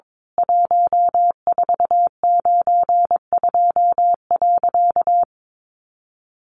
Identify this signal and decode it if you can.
Morse code, heard as a keyed tone, about 22 wpm, 700 Hz. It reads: 1492.
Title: A Sound Of Thunder 22wpm